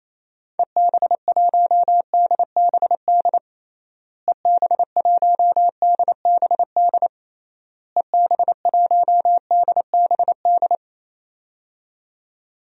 Morse code, heard as a keyed tone, about 28 wpm, 700 Hz. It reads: EB1DBD EB1DBD EB1DBD
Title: DXCC-100 Call Signs - 6 Characters - Repeated 3x - 28wpm